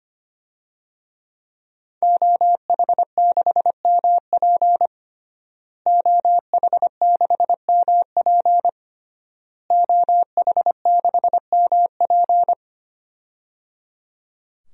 Morse code, heard as a keyed tone, about 25 wpm, 700 Hz. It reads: OH6MP OH6MP OH6MP
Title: DXCC-100 Call Signs - 5 Characters - Repeated 3x - 25wpm